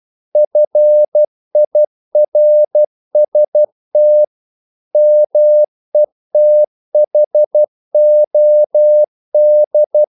Morse code, heard as a keyed tone, about 12 wpm, 600 Hz. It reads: FIRST METHOD
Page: English quotations in CW (telegraphy - morse)